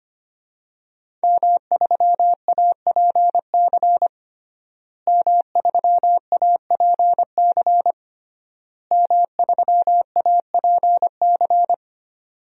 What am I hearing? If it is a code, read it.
Morse code, heard as a keyed tone, about 25 wpm, 700 Hz. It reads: M3APC M3APC M3APC
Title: DXCC-100 Call Signs - 5 Characters - Repeated 3x - 25wpm